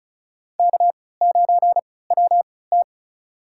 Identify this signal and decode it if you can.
Morse code, heard as a keyed tone, about 36 wpm, spaced as for 20 wpm, 700 Hz. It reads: K9WT
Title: USA Call Signs - 3x Character Spacing - 35wpm